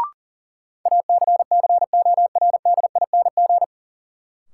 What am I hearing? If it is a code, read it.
Morse code, heard as a keyed tone, about 37 wpm, 700 Hz. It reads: ACCORDING
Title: Sentences from Top 700 Words - New Word Drill - 40wpm